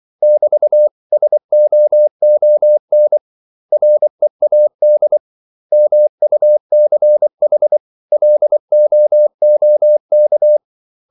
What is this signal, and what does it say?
Morse code, heard as a keyed tone, about 24 wpm, 600 Hz. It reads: <BT>SOON READ MUCH LOOK
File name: CW / Practice / 500MostCommonEnglishWords24